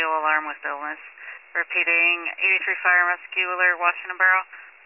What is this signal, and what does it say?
Emergency Alert System (SAME) tones, heard as harsh buzzing bursts, with speech. It is heard from 1.2 to 4.4 s.
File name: pager / 83 RESCUE DUTY / wcalert